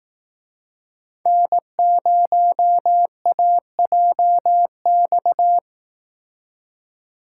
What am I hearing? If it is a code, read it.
Morse code, heard as a keyed tone, about 18 wpm, 700 Hz. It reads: N0AJX
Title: Speed Racing - USA 1x3 Call Signs 15wpm